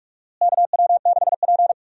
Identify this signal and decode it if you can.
Morse code, heard as a keyed tone, about 46 wpm, 700 Hz. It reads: KW6P
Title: USA Call Signs - 4 Characters - Repeated 2x - 45wpm